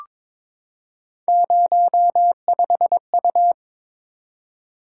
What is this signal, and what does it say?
Morse code, heard as a keyed tone, about 22 wpm, 700 Hz. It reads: 05U
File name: practice / 22wpm / Three Letter-Number Combinations - 22wpm